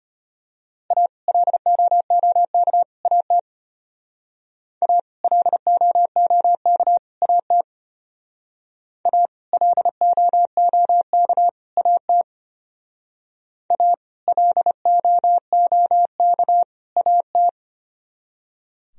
Morse code, heard as a keyed tone, about 32 wpm, 700 Hz. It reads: A LOOK AT A LOOK AT A LOOK AT A LOOK AT